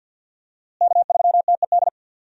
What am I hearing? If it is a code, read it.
Morse code, heard as a keyed tone, about 51 wpm, 700 Hz. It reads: K3TED
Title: USA Call Signs - 5 Characters - Repeated 3x - 50wpm